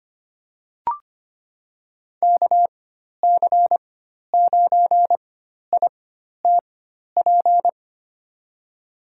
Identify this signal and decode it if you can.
Morse code, heard as a keyed tone, about 25 wpm, spaced as for 12 wpm, 700 Hz. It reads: KC9ITP
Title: USA Call Signs - 6 Characters - 4x Character Spacing - 25wpm